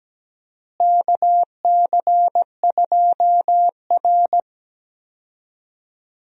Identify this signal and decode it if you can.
Morse code, heard as a keyed tone, about 17 wpm, 700 Hz. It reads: KC2R
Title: USA Call Signs - 4 Characters - Repeated 2x - 17wpm